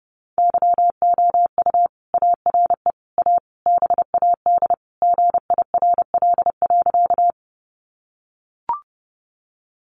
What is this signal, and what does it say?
Morse code, heard as a keyed tone, about 30 wpm, 700 Hz. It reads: YOU ARE A BAD GIRL.
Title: Sentences from Top 700 Words 30wpm